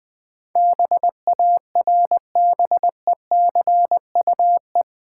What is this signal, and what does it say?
Morse code, heard as a keyed tone, about 22 wpm, 700 Hz. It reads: BARBECUE
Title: Easy Words - Speed-Racing with Hints - 20wpm